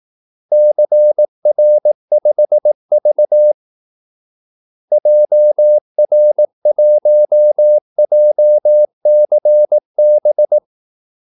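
Morse code, heard as a keyed tone, about 18 wpm, 600 Hz. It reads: CR5V JR1JCB
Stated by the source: Morse call-sign practice at 18 wpm.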